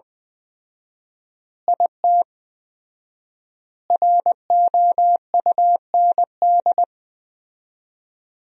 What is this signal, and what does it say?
Morse code, heard as a keyed tone, about 21 wpm, 700 Hz. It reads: IT ROUND
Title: 4x Word-Spacing - Sentences from Top 500 Words - 20wpm